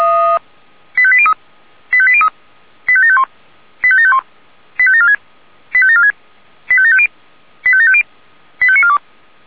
Five-tone selective call (ZVEI), heard as quick runs of five tones, tones of about 70 ms. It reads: 85803 85803 85751 85751 85746 85746 85750 85750 85942